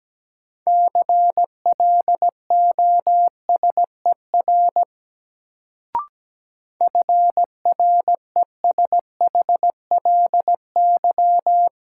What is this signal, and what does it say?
Morse code, heard as a keyed tone, about 17 wpm, 700 Hz. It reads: CLOSER FRESHLY
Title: Sets of 5 Words Encore - New Word Drill - 17wpm